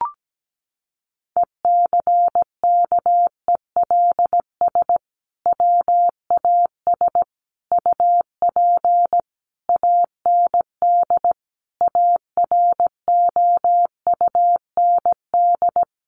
Morse code, heard as a keyed tone, about 17 wpm, 700 Hz. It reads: ECKELS WAS UP AND AROUND
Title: A Sound Of Thunder 17wpm